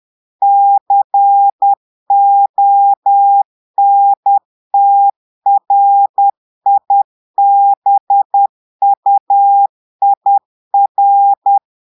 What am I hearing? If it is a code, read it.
Morse code, heard as a keyed tone, about 10 wpm, 800 Hz. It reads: CONTRIBUIR
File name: Portugais - texte en clair 1 - vitesse - 10 mots minute